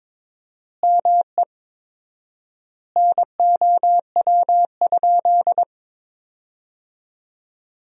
Morse code, heard as a keyed tone, about 22 wpm, 700 Hz. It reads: ME NOW?
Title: 4x Word-Spacing - Sentences from Top 400 Words - 22wpm